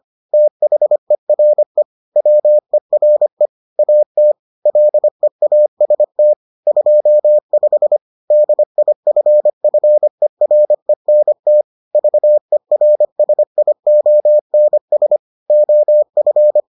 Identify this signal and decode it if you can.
Morse code, heard as a keyed tone, about 25 wpm, 600 Hz. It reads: THERE WERE AT LEAST 25 DIFFERENT VERSIONS OF